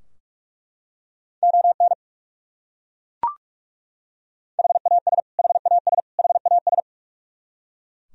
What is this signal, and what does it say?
Morse code, heard as a keyed tone, about 41 wpm, 700 Hz. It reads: ON HAS HAS HAS
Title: Top 100 Words Repeated 45wpm